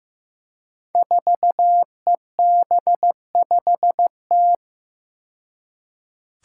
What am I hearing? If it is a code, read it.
Morse code, heard as a keyed tone, about 15 wpm, 700 Hz. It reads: 4EB5T
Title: Five Challenging Letter-Number Combinations 15wpm